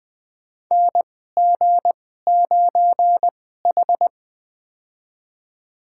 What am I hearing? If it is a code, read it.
Morse code, heard as a keyed tone, about 20 wpm, spaced as for 14 wpm, 700 Hz. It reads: NG9H
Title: USA Call Signs - 4 Characters - 2x Character Spacing - 20wpm